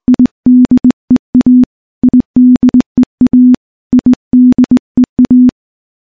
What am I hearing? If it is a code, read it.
Morse code, heard as a keyed tone, about 20 wpm, 250 Hz. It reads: IDEA IDEA IDEA
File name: morse / morse-training-20WPM